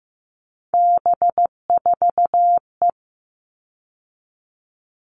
Morse code, heard as a keyed tone, about 15 wpm, 700 Hz. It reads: B4E
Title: All Cut Numbers - 3-Digit 15wpm